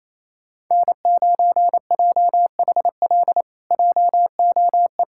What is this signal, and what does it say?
Morse code, heard as a keyed tone, about 27 wpm, 700 Hz. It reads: N9JHL JOE
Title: K1USN SST QSO Practice - 28wpm